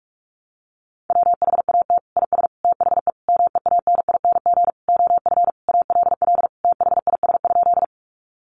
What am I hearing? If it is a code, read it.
Morse code, heard as a keyed tone, about 44 wpm, 700 Hz. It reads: WHAT IS THE MEANING OF ALL THIS?
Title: Sentences from Top 900 Words 45wpm